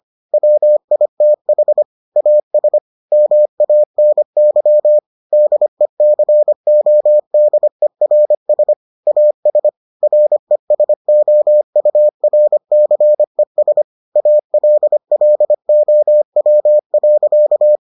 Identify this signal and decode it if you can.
Morse code, heard as a keyed tone, about 25 wpm, 600 Hz. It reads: WITH AS MANY DECODERS AS RESOURCES ALLOW.